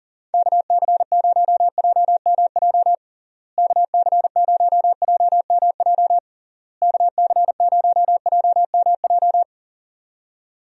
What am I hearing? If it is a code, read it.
Morse code, heard as a keyed tone, about 40 wpm, 700 Hz. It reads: KC0JMJ KC0JMJ KC0JMJ
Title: USA Call Signs - 6 Characters - Repeated 3x - 40wpm